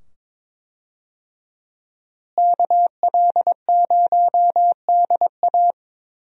Morse code, heard as a keyed tone, about 22 wpm, 700 Hz. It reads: KL0DA
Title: USA Call Signs - 5 Characters - Repeated 2x - 22wpm